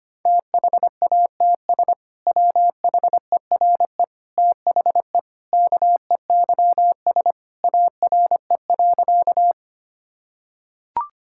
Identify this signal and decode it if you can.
Morse code, heard as a keyed tone, about 25 wpm, 700 Hz. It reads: THATS WHERE THE KEYS ARE.
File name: practice / 25wpm / Sentences with 2-Word Phrases - 25wpm